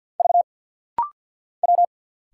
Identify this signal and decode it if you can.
Morse code, heard as a keyed tone, about 50 wpm, 700 Hz. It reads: V W
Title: Single Letters - Rapid-Fire 50wpm